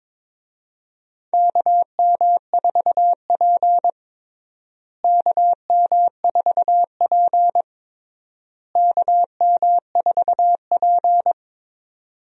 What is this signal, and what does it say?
Morse code, heard as a keyed tone, about 22 wpm, 700 Hz. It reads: KM4P KM4P KM4P
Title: USA Call Signs - 4 Characters - Repeated 3x - 22wpm